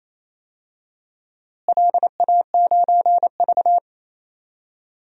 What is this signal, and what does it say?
Morse code, heard as a keyed tone, about 28 wpm, 700 Hz. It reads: LA9V
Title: DXCC-100 Call Signs - 4 Characters - Repeated 3x - 28wpm